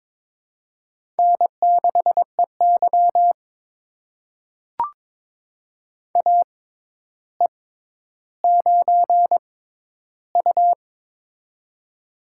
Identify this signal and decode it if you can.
Morse code, heard as a keyed tone, about 22 wpm, spaced as for 11 wpm, 700 Hz. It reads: N6EY AE9U
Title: USA Call Signs - 4 Characters - 6x Character Spacing - 22wpm